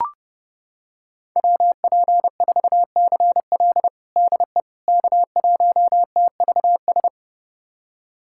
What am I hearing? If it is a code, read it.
Morse code, heard as a keyed tone, about 30 wpm, 700 Hz. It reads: WP4CL DE K1TVS
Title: USA Callsign DE Callsign - 5 Characters - 30wpm